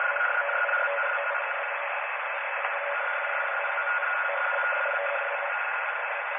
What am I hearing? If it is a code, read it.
Two-tone teleprinter signal, about 850 Hz shift.